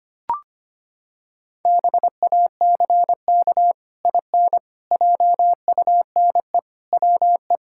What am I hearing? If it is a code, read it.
Morse code, heard as a keyed tone, about 25 wpm, 700 Hz. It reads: BACK IN JUNE WE
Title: Pangram Challenge v2 - 25wpm